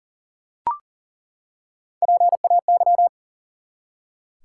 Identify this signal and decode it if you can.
Morse code, heard as a keyed tone, about 40 wpm, 700 Hz. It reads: PAY
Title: Three Letter Words - 40wpm